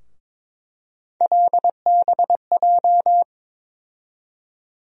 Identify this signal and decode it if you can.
Morse code, heard as a keyed tone, about 22 wpm, 700 Hz. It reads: LBJ
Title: USA Call Sign Postfixes - 3 Letters 22wpm